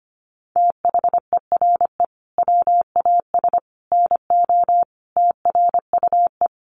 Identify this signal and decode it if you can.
Morse code, heard as a keyed tone, about 25 wpm, 700 Hz. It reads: THERE WAS NO TRUE
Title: Sentences from Top 1300 Words 25wpm